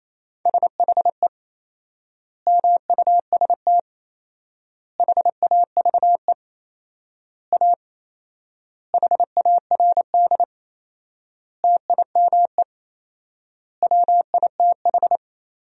Morse code, heard as a keyed tone, about 28 wpm, 700 Hz. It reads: SHE MUST HAVE A HARD TIME WITH
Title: 4x Word-Spacing - Sentences from Top 300 Words - 28wpm